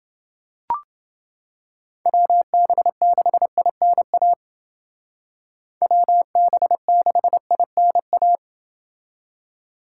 Morse code, heard as a keyed tone, about 29 wpm, 700 Hz. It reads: WB6INA WB6INA